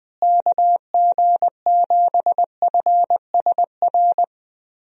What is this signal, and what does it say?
Morse code, heard as a keyed tone, about 21 wpm, 700 Hz. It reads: KG7FSR
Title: USA Call Signs - 6 Characters - Repeated 2x - 20wpm